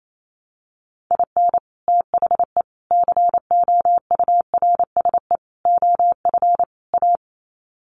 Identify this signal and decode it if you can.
Morse code, heard as a keyed tone, about 28 wpm, 700 Hz. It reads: IN THE COURSE OF A
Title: Sets of 5 Words Encore2 28wpm